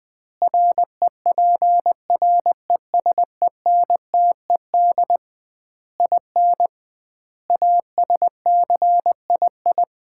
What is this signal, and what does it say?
Morse code, heard as a keyed tone, about 20 wpm, 700 Hz. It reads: REPRESENTED IN ASCII
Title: 2x Word-Spacing - Sets of 5 Words - 20wpm